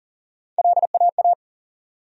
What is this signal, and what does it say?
Morse code, heard as a keyed tone, about 40 wpm, 700 Hz. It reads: LAA
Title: USA Call Sign Postfixes - 3 Letters 40wpm